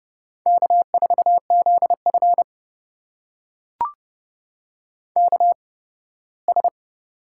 Morse code, heard as a keyed tone, about 32 wpm, spaced as for 18 wpm, 700 Hz. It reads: K4ZF KS
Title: USA Call Signs - 4 Characters - 8x Character Spacing - 30wpm